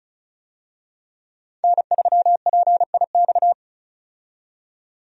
Morse code, heard as a keyed tone, about 36 wpm, 700 Hz. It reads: N3PIX
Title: USA Call Signs - 5 Characters - 35wpm